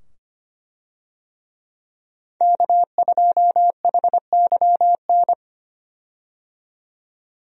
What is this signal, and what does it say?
Morse code, heard as a keyed tone, about 25 wpm, 700 Hz. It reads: K2HYN
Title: USA Call Signs - 5 Characters - 25wpm